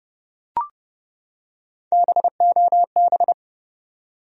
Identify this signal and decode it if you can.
Morse code, heard as a keyed tone, about 30 wpm, 700 Hz. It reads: BOB
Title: USA Common Names - 30wpm